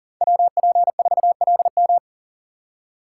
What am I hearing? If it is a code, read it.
Morse code, heard as a keyed tone, about 40 wpm, 700 Hz. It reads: WP4LM
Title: USA Call Signs - 5 Characters - 2x Character Spacing - 40wpm